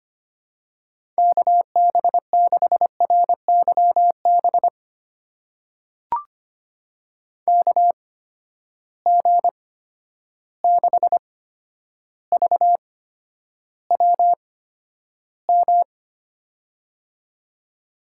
Morse code, heard as a keyed tone, about 25 wpm, spaced as for 11 wpm, 700 Hz. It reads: KB6RYB KG6VWM
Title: USA Call Signs - 6 Characters - 8x Character Spacing - 25wpm